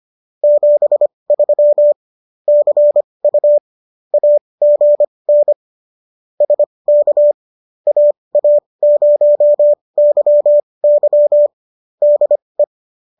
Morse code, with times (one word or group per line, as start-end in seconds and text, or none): 0.4-1.9 73
2.5-3.6 CU
4.1-5.5 AGN
6.4-7.3 SK
7.9-11.5 AA0YY
12.0-12.6 DE